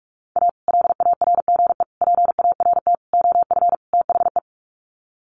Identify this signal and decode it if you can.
Morse code, heard as a keyed tone, about 44 wpm, 700 Hz. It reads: A LARGE PART OF THE
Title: Sets of 5 Words Encore2 45wpm